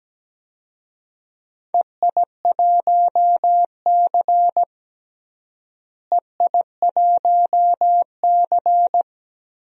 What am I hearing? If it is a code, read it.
Morse code, heard as a keyed tone, about 17 wpm, 700 Hz. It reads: EI1C EI1C
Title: DXCC-100 Call Signs - 4 Characters - Repeated 2x - 17wpm